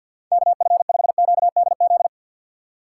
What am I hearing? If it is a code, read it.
Morse code, heard as a keyed tone, about 49 wpm, 700 Hz. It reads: KF5QDZ
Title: USA Call Signs - 6 Characters - 4x Character Spacing - 50wpm